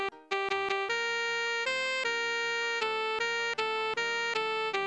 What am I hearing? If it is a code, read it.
There is no code.